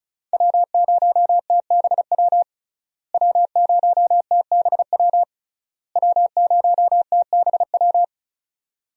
Morse code, heard as a keyed tone, about 36 wpm, 700 Hz. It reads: W0TBW W0TBW W0TBW
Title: USA Call Signs - 5 Characters - Repeated 3x - 35wpm